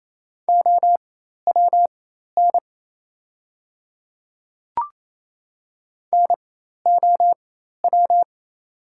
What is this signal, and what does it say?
Morse code, heard as a keyed tone, about 28 wpm, spaced as for 13 wpm, 700 Hz. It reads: OWN NOW
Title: 4x Character Spacing - Top 300 Words - 28wpm